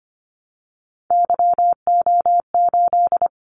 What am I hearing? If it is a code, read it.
Morse code, heard as a keyed tone, about 25 wpm, 700 Hz. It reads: YO8
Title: Callsign Prefixes for Top 25 DXCC - 25wpm